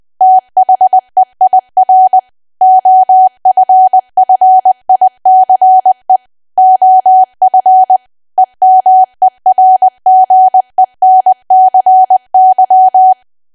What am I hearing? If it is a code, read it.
Morse code, heard as a keyed tone, about 20 wpm, 750 Hz. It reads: THEIR OFFICE OF EMERGENCY
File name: Archive / 20 WPM / 20WPM